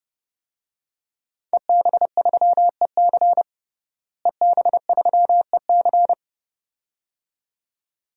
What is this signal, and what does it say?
Morse code, heard as a keyed tone, about 29 wpm, 700 Hz. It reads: EB3EC EB3EC
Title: DXCC-100 Call Signs - 5 Characters - Repeated 2x - 30wpm